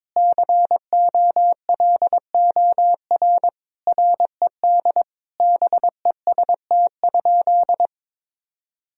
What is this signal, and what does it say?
Morse code, heard as a keyed tone, about 22 wpm, 700 Hz. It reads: COLOR RED BEST?
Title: Sentences from Top 1000 Words - Review Part 2 - 22wpm